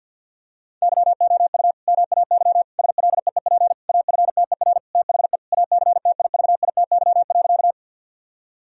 Morse code, heard as a keyed tone, about 50 wpm, 700 Hz. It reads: YOU MAY SLEEP AFTER THE ACTIVITY.
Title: Sentences from Top 1000 Words - Review Part 4 - 50wpm